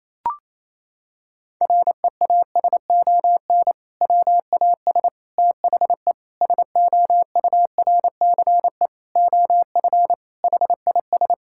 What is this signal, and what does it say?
Morse code, heard as a keyed tone, about 28 wpm, 700 Hz. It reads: REASON WAS THE SOURCE OF HIS
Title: Sentences from Top 1000 Words - Review Part 5 - 28wpm